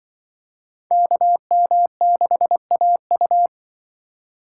Morse code, heard as a keyed tone, about 25 wpm, 700 Hz. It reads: KM6AU
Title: Speed Racing - USA 2x2 Call Signs 20wpm